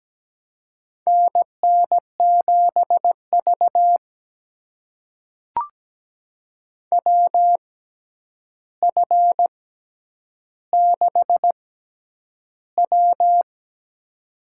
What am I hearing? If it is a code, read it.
Morse code, heard as a keyed tone, about 17 wpm, spaced as for 9 wpm, 700 Hz. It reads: NN7V WF6W
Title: USA Call Signs - 4 Characters - 6x Character Spacing - 17wpm